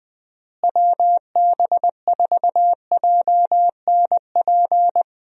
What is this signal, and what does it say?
Morse code, heard as a keyed tone, about 20 wpm, 700 Hz. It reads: WB4JNP
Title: USA Call Signs - 6 Characters - Repeated 3x - 20wpm